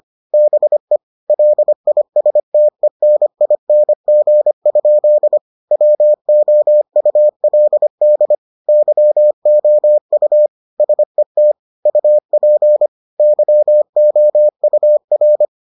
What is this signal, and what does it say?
Morse code, heard as a keyed tone, about 25 wpm, 600 Hz. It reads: BE LISTENING? WOULD YOU SET UP YOUR